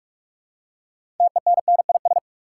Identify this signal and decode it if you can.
Morse code, heard as a keyed tone, about 43 wpm, 700 Hz. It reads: TENNIS
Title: Six Letter Words - 45wpm